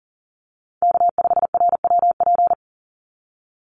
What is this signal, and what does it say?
Morse code, heard as a keyed tone, about 40 wpm, 700 Hz. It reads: K5RWP
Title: NA SOTA Chaser Callsigns 40wpm